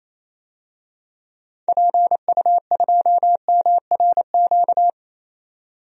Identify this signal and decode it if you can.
Morse code, heard as a keyed tone, about 28 wpm, 700 Hz. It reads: PU2MRQ
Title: DXCC-100 Call Signs - 6 Characters - 28wpm